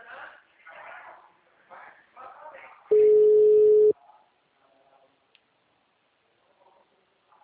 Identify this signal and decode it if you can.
Ringback tone (Europe, 425 Hz), with speech.